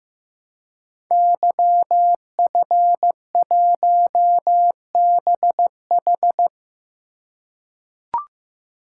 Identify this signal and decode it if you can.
Morse code, heard as a keyed tone, about 15 wpm, 700 Hz. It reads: YF1BH
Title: Code Groups - 5 Letter:Numbers - 15wpm